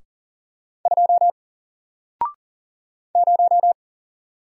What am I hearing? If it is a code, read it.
Morse code, heard as a keyed tone, about 38 wpm, 700 Hz. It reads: 2 0
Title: Single Character (Letter:Number:Punctuation:Prosign) 40wpm